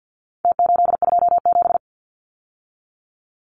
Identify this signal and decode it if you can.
Morse code, heard as a keyed tone, about 50 wpm, 700 Hz. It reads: T827
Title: Numbers - Contest Style - 4-Digit 50wpm